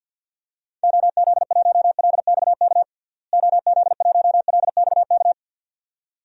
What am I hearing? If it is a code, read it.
Morse code, heard as a keyed tone, about 47 wpm, 700 Hz. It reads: OZ1LXK OZ1LXK
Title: DXCC-100 Call Signs - 6 Characters - Repeated 2x - 50wpm